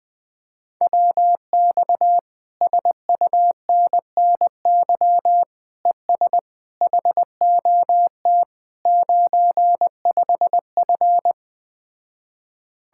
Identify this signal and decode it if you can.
Morse code, heard as a keyed tone, about 20 wpm, 700 Hz. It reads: WX SUNNY ES HOT 95F
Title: Speed-Racing Common QSO Phrases - 17wpm